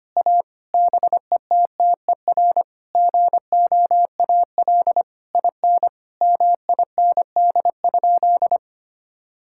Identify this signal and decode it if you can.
Morse code, heard as a keyed tone, about 25 wpm, 700 Hz. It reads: A BETTER GOAL IN MIND?
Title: Sentences from Top 1000 Words - Review Part 5 - 25wpm